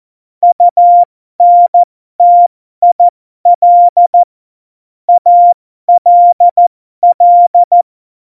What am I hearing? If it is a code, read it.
Morse code, heard as a keyed tone, about 14 wpm, spaced as for 12 wpm, 700 Hz. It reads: UNTIL ALL
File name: Morsekurs / Witze